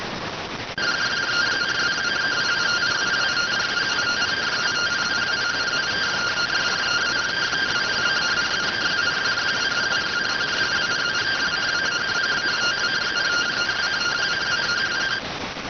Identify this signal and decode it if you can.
Two-tone teleprinter signal, about 170 Hz shift.